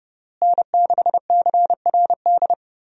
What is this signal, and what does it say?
Morse code, heard as a keyed tone, about 30 wpm, 700 Hz. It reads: N6CRD
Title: USA Call Signs - 5 Characters - Repeated 3x - 30wpm